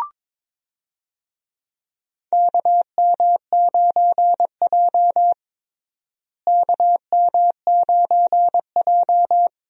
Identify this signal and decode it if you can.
Morse code, heard as a keyed tone, about 22 wpm, 700 Hz. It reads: KM9J KM9J
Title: USA Call Signs - 4 Characters - Repeated 3x - 22wpm